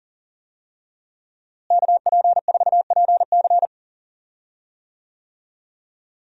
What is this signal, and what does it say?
Morse code, heard as a keyed tone, about 37 wpm, 700 Hz. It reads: KP4PC
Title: DXCC-100 Call Signs - 5 Characters - Repeated 3x - 40wpm